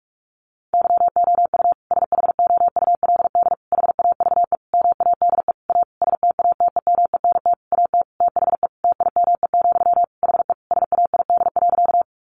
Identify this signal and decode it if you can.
Morse code, heard as a keyed tone, about 44 wpm, 700 Hz. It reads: YOU SHOULD HAVE MADE A STATEMENT AT THE TIME, HE SAID.
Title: Sentences from Top 1200 Words 45wpm